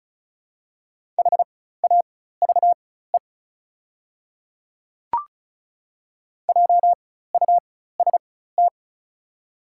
Morse code, heard as a keyed tone, about 34 wpm, spaced as for 16 wpm, 690 Hz. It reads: HAVE JUST
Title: 4x Character Spacing - Top 200 Words - 35wpm